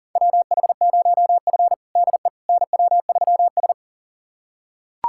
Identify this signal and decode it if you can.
Morse code, heard as a keyed tone, about 40 wpm, 700 Hz. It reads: WH0F DE NW3S
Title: USA Callsign DE Callsign - 4 Characters - 40wpm